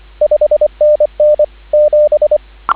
Morse code, heard as a keyed tone, about 24 wpm, 590 Hz, with radio noise. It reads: 5NN 7